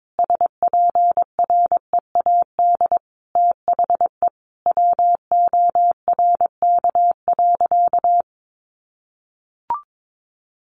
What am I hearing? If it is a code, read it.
Morse code, heard as a keyed tone, about 22 wpm, 700 Hz. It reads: SPREAD THE WORK.
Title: Sentences from Top 1400 Words 22wpm